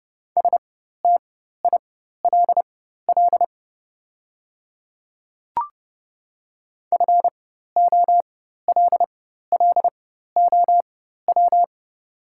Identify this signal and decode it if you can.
Morse code, heard as a keyed tone, about 30 wpm, spaced as for 14 wpm, 700 Hz. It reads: STILL FOLLOW
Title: 4x Character Spacing - Top 300 Words - 30wpm